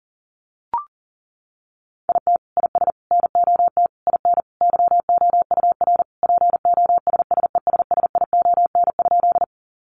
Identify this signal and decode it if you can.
Morse code, heard as a keyed tone, about 40 wpm, 700 Hz. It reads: IT IS NOT IN YOUR POSSESSION?
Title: Sentences from Top 1400 Words 40wpm